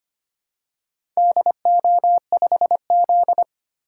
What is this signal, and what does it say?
Morse code, heard as a keyed tone, about 25 wpm, 700 Hz. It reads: DO5Z
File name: practice / 25wpm / Four Letter-Number Combinations - 25wpm